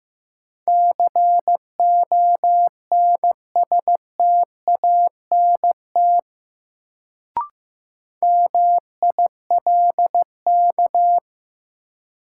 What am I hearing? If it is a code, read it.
Morse code, heard as a keyed tone, about 15 wpm, 700 Hz. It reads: CONSTANT MILK
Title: Sentences from Top 1500 Words - New Word Drill - 15wpm